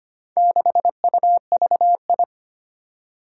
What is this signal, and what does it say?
Morse code, heard as a keyed tone, about 25 wpm, 700 Hz. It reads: 6UVI
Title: Four Challenging Letter-Number Combinations 25wpm